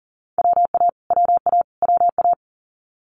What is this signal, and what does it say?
Morse code, heard as a keyed tone, about 40 wpm, 700 Hz. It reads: WA WA WA
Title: US States Repeated 3x 40wpm